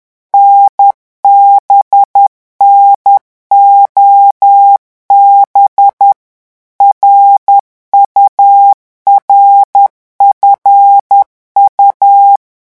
Morse code, heard as a keyed tone, about 10 wpm, 800 Hz. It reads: NBNOB RURFU
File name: morse mp3 / revision11 lettres vitesse 10